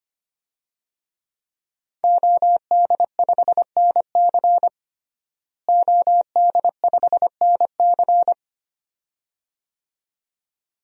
Morse code, heard as a keyed tone, about 25 wpm, 700 Hz. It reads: OD5NC OD5NC
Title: DXCC-100 Call Signs - 5 Characters - Repeated 2x - 25wpm